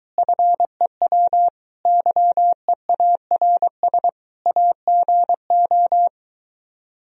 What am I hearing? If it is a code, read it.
Morse code, heard as a keyed tone, about 23 wpm, 700 Hz. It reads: FEW YEARS AGO